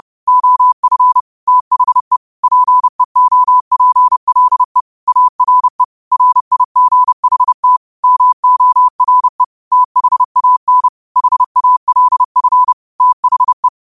Morse code, heard as a keyed tone, about 30 wpm, 1000 Hz. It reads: OF THE PEOPLE ARE RIGHT MORE THAN HALF THE
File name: 30wpm